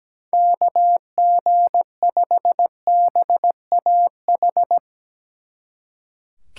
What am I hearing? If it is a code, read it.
Morse code, heard as a keyed tone, about 18 wpm, 700 Hz. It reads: KG5BAH
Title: Speed Racing - USA 2x3 Call Signs 17wpm